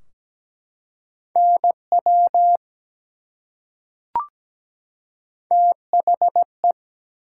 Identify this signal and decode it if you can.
Morse code, heard as a keyed tone, about 17 wpm, 700 Hz. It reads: NW THE
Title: Top 180 QSO Elements 17wpm